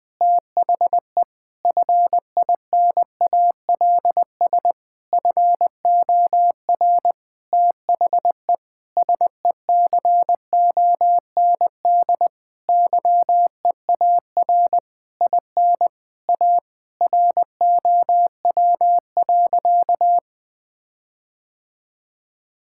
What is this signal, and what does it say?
Morse code, heard as a keyed tone, about 20 wpm, 700 Hz. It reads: THE FINALS FOR THE SECOND YEAR IN A ROW.
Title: Sentences with 7-Word Phrases - 20wpm